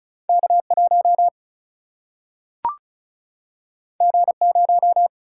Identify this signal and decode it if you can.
Morse code, heard as a keyed tone, about 36 wpm, 700 Hz. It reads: K1 G0
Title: Single-Letter Single-Number 35wpm